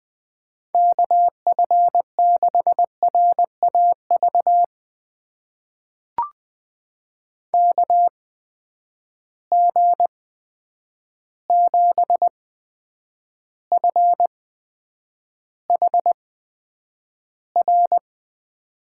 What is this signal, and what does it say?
Morse code, heard as a keyed tone, about 20 wpm, spaced as for 9 wpm, 700 Hz. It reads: KF6RAV KG7FHR
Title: USA Call Signs - 6 Characters - 8x Character Spacing - 20wpm